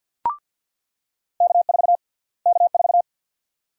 Morse code, heard as a keyed tone, about 48 wpm, 700 Hz. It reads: K4 K4
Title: USA Call Sign Prefixes - Repeated 2x - 50wpm